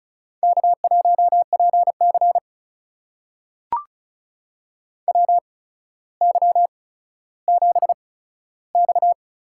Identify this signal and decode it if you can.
Morse code, heard as a keyed tone, about 36 wpm, spaced as for 15 wpm, 700 Hz. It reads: K1PC WY7X
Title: USA Call Signs - 4 Characters - 8x Character Spacing - 35wpm